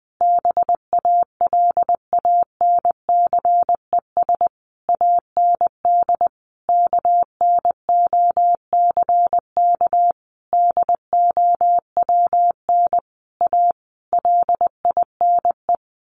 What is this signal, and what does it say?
Morse code, heard as a keyed tone, about 20 wpm, 700 Hz. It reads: BALANCES AND KNOCK DOWN A LINE
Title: A Sound Of Thunder 20wpm